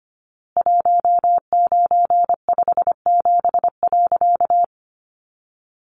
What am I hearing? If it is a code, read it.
Morse code, heard as a keyed tone, about 25 wpm, 700 Hz. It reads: 1957.
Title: A Sound Of Thunder 25wpm